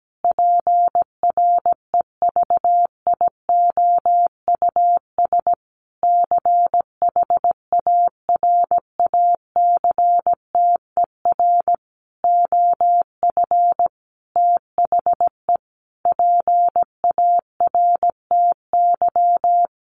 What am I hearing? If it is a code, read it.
Morse code, heard as a keyed tone, about 17 wpm, 700 Hz. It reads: PREVIOUS CHARACTER OF THE PARTY
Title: Sentences from Top 1500 Words 17wpm